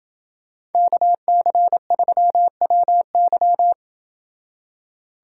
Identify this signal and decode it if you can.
Morse code, heard as a keyed tone, about 29 wpm, 700 Hz. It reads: KC3WY